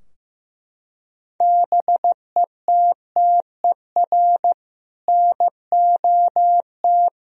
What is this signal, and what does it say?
Morse code, heard as a keyed tone, about 15 wpm, 700 Hz. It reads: BETTER NOT
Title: Sentences from Top 1000 Words - Review Part 4 - 15wpm